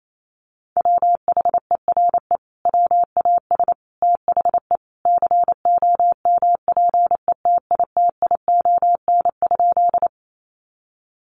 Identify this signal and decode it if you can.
Morse code, heard as a keyed tone, about 28 wpm, 700 Hz. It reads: WHERE WAS THE COMPETITION?
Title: Sentences from Top 1000 Words 28wpm